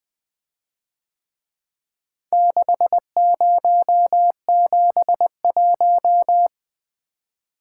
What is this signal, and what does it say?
Morse code, heard as a keyed tone, about 22 wpm, 700 Hz. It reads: 6071
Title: Speed-Racing - Four Letter-Number Combinations - 20wpm